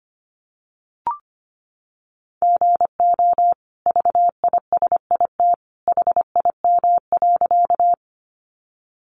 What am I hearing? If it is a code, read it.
Morse code, heard as a keyed tone, about 25 wpm, 700 Hz. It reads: GO VISIT HIM.
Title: Sentences from Top 900 Words 25wpm